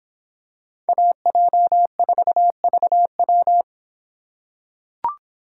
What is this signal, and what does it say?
Morse code, heard as a keyed tone, about 24 wpm, 700 Hz. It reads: AJ4VW